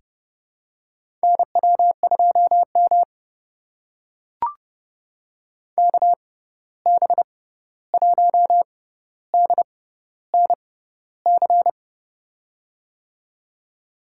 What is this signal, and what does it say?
Morse code, heard as a keyed tone, about 30 wpm, spaced as for 14 wpm, 700 Hz. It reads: NW2M KB1DNC
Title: USA Call Signs - 6x Character Spacing - 30wpm